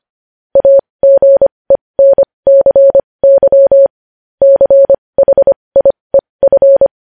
Morse code, heard as a keyed tone, about 25 wpm, spaced as for 20 wpm, 550 Hz. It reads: AGENCY CHIEF
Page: News headlines in morse code at 20 wpm